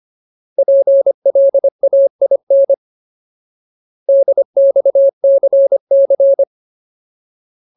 Morse code, heard as a keyed tone, about 25 wpm, 550 Hz. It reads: PLAIN DXCC
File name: Ultimate Word List / 25WPM Ultimate